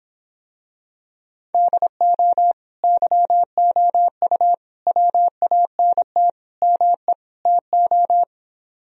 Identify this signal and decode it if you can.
Morse code, heard as a keyed tone, about 26 wpm, 700 Hz. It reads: DO YOU WANT ME TO